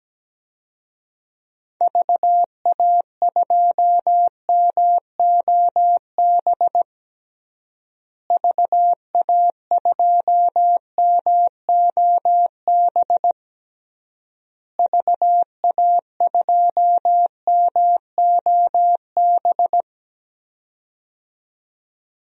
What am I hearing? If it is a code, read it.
Morse code, heard as a keyed tone, about 17 wpm, 700 Hz. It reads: VA2MOB VA2MOB VA2MOB
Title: DXCC-100 Call Signs - 6 Characters - Repeated 3x - 17wpm